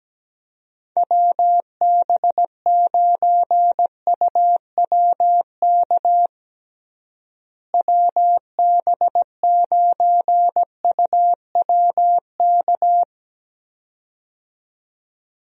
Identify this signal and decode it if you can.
Morse code, heard as a keyed tone, about 17 wpm, 700 Hz. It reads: WB9UWK WB9UWK
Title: USA Call Signs - 6 Characters - Repeated 2x - 17wpm